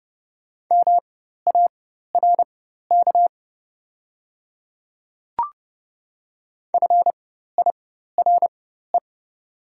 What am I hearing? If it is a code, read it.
Morse code, heard as a keyed tone, about 30 wpm, spaced as for 14 wpm, 700 Hz. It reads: MARK FIRE
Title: 4x Character Spacing - Top 500 Words - 30wpm